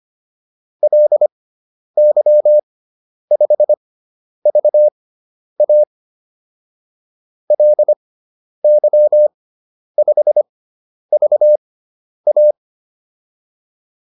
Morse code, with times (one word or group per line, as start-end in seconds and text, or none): 0.8-5.8 LY5VA
7.5-12.5 LY5VA